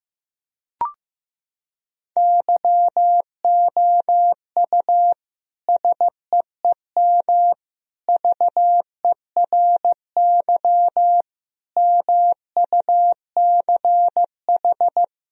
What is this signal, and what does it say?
Morse code, heard as a keyed tone, about 15 wpm, 700 Hz. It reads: YOU SEEM VERY MUCH
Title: Sentences from Top 1000 Words - Review Part 1 - 15wpm